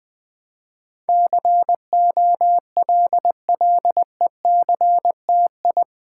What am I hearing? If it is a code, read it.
Morse code, heard as a keyed tone, about 20 wpm, 700 Hz. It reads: COLLECTI
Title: Build A Word - 4 Letters At A Time - 10-Letter Words - 20wpm